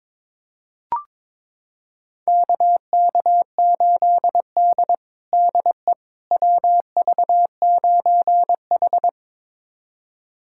Morse code, heard as a keyed tone, about 22 wpm, 700 Hz. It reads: KK8D DE WV9H
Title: USA Callsign DE Callsign - 4 Characters - 22wpm